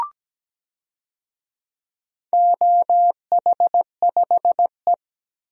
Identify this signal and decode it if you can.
Morse code, heard as a keyed tone, about 17 wpm, 700 Hz. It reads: OH5E
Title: DXCC-100 Call Signs - 4 Characters - 17wpm